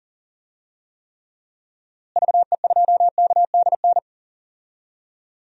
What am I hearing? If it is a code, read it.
Morse code, heard as a keyed tone, about 37 wpm, 700 Hz. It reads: VE2KDN
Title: DXCC-100 Call Signs - 6 Characters - Repeated 3x - 40wpm